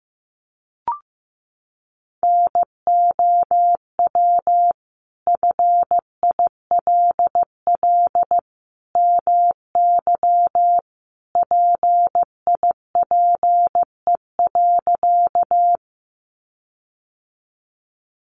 Morse code, heard as a keyed tone, about 15 wpm, 700 Hz. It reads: NOW FILL MY PIPE.
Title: Sentences from Top 1500 Words 15wpm